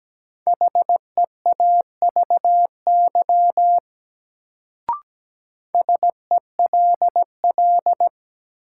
Morse code, heard as a keyed tone, about 17 wpm, 700 Hz. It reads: HEAVY SELL
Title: Sentences from Top 1000 Words - Review Part 6 - New Word Drill - 17wpm